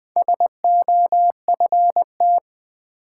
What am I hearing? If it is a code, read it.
Morse code, heard as a keyed tone, about 20 wpm, 700 Hz. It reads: SOFT
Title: Sentences from Top 1500 Words - New Word Drill Alt - 20wpm